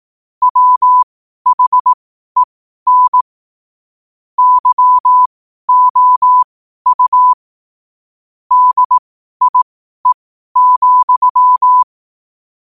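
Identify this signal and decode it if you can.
Morse code, heard as a keyed tone, about 18 wpm, spaced as for 12 wpm, 1000 Hz. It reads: WHEN YOU DIE,
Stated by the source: Morse code at 12 wpm.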